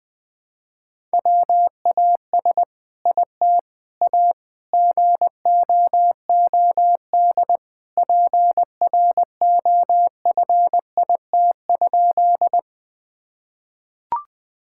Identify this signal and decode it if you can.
Morse code, heard as a keyed tone, about 20 wpm, 700 Hz. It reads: WAS IT A GOOD PROFIT?
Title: Sentences from Top 1000 Words - Review Part 5 - 20wpm